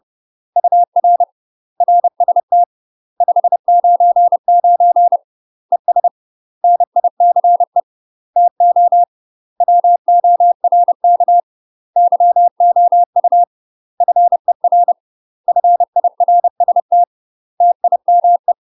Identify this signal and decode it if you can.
Morse code, heard as a keyed tone, about 30 wpm, 700 Hz. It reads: UR RST 599 ES NICE TO WORK YOU FER FIRST TIME
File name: UK MASTER QSOs TRAINING / 30WPM / UK MASTER QSOs 0018